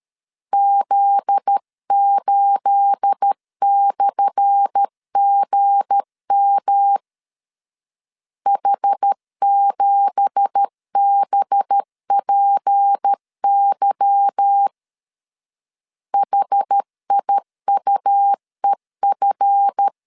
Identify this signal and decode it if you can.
Morse code, heard as a keyed tone, about 13 wpm, 800 Hz. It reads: Z8/GM H7BPY HIUEF
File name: LEZIONI / 10 SIGNS / 10A SLASH 12wpm 2314